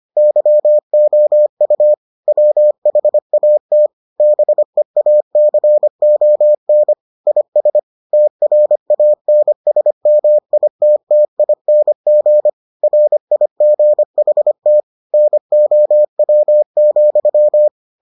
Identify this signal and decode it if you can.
Morse code, heard as a keyed tone, about 25 wpm, 600 Hz. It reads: YOU WHAT BEACON IS TRANSMITTING RIGHT NOW,